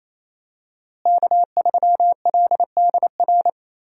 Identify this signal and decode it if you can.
Morse code, heard as a keyed tone, about 28 wpm, 700 Hz. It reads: K3LDR
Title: USA Call Signs - 5 Characters - Repeated 3x - 28wpm